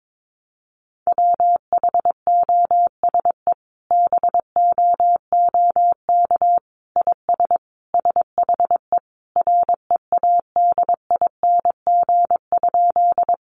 Morse code, heard as a keyed tone, about 22 wpm, 700 Hz. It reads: WHOSE BOOK IS SHE READING?
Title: Sentences from Top 1200 Words 22wpm